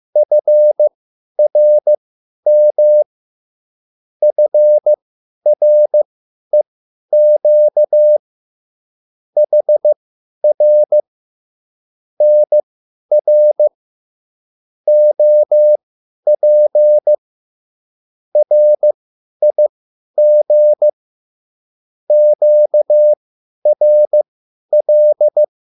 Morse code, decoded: FRM FREQ HR NR OP RIG QRL

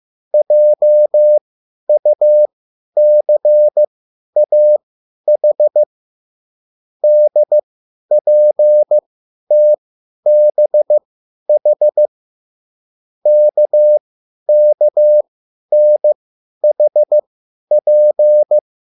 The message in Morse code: JUCAH DPTBH KKNHP

